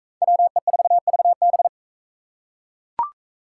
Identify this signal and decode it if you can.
Morse code, heard as a keyed tone, about 42 wpm, 700 Hz. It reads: WE4VB